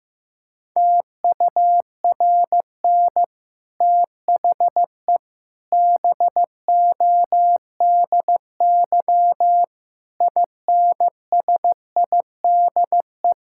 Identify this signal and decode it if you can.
Morse code, heard as a keyed tone, about 15 wpm, 700 Hz. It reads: TURN THE BODY INSIDE
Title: Sentences from Top 1000 Words - Review Part 6 - 15wpm